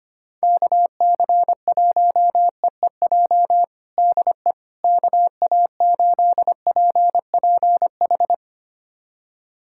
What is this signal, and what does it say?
Morse code, heard as a keyed tone, about 25 wpm, 700 Hz. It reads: KC1EEJ DE KA8PPH
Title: USA Callsign DE Callsign - 6 Characters - 25wpm